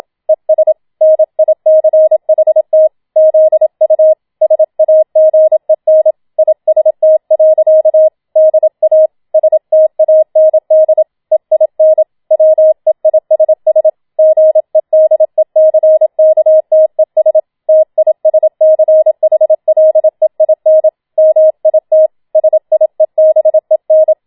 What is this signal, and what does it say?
Morse code, heard as a keyed tone, about 27 wpm, 630 Hz. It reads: ES NICHT ZU SAGEN IST. DA STAND EIN WEISS GEDECKTES TISCHLEIN MIT SIEBEN
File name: Schneewittchen Teil2 27wpm